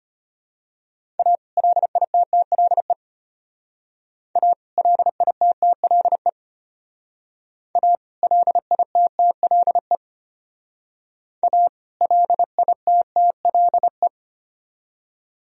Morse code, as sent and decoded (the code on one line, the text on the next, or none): .- / .-.. .. - - .-.. / .- / .-.. .. - - .-.. / .- / .-.. .. - - .-.. / .- / .-.. .. - - .-.. .
A LITTLE A LITTLE A LITTLE A LITTLE